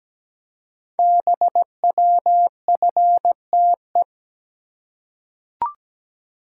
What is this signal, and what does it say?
Morse code, heard as a keyed tone, about 17 wpm, 700 Hz. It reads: BWFTE
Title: Code Groups - 5 Letters - 17wpm